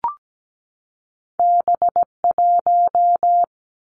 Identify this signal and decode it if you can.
Morse code, heard as a keyed tone, about 17 wpm, 700 Hz. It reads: B 1
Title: Two Challenging Letter-Number Combinations 17wpm